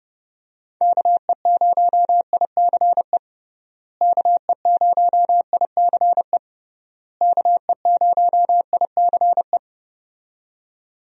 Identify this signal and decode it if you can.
Morse code, heard as a keyed tone, about 30 wpm, 700 Hz. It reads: KE0ICE KE0ICE KE0ICE
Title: USA Call Signs - 6 Characters - Repeated 3x - 30wpm